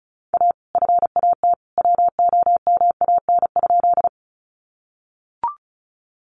Morse code, heard as a keyed tone, about 34 wpm, 700 Hz. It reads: A FAT WOMAN?
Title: Sentences from Top 1100 Words 35wpm